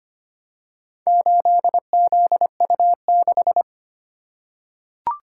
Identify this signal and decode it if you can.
Morse code, heard as a keyed tone, about 25 wpm, 700 Hz. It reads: 8ZU6
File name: practice / 25wpm / Four Letter-Number Combinations - 25wpm